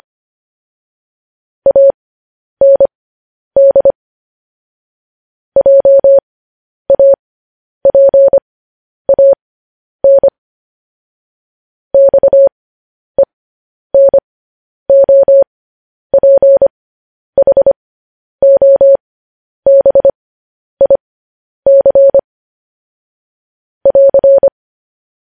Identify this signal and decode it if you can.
Morse code, heard as a keyed tone, about 25 wpm, spaced as for 10 wpm, 550 Hz. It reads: AND JAPAN XENOPHOBIC <AR>
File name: podcast / News headlines-1x-25x10